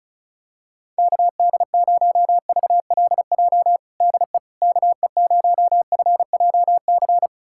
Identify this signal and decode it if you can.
Morse code, heard as a keyed tone, about 34 wpm, 700 Hz. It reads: KD0VLJ DE KE0FJC
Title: USA Callsign DE Callsign - 6 Characters - 35wpm